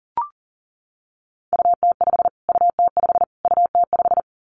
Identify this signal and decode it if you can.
Morse code, heard as a keyed tone, about 40 wpm, 700 Hz. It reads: UT5 UT5 UT5
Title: Callsign Prefixes for Top 25 DXCC - Repeated 3x - 40wpm